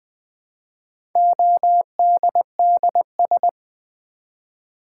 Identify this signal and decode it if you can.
Morse code, heard as a keyed tone, about 20 wpm, 700 Hz. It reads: ODDS
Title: Build A Word - 1 Letter At A Time - 4-Letter Words - 20wpm